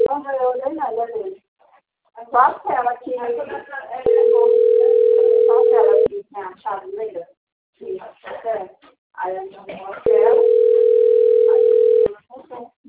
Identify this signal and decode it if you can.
Ringback tone (North America), 440 + 480 Hz, with speech.
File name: out- - - - - .94216